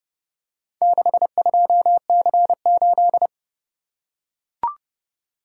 Morse code, heard as a keyed tone, about 32 wpm, 700 Hz. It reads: 62C8